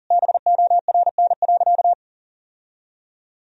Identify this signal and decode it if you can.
Morse code, heard as a keyed tone, about 40 wpm, 700 Hz. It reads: BORN.
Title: Sentences from Top 1000 Words - Review Part 2 - 40wpm